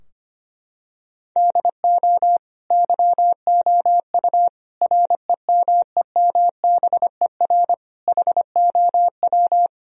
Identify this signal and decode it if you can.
Morse code, heard as a keyed tone, about 25 wpm, 700 Hz. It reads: DO YOU REMEMBER HOW
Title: Sentences from Top 1000 Words - Review Part 3 - 25wpm